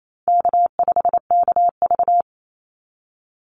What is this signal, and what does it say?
Morse code, heard as a keyed tone, about 28 wpm, 700 Hz. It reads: K5KV
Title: Easy Call Signs Repeated 3x - 28wpm